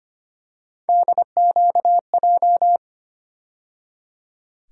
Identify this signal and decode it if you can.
Morse code, heard as a keyed tone, about 25 wpm, 700 Hz. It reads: DQJ
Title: USA Call Sign Postfixes - 3 Letters 25wpm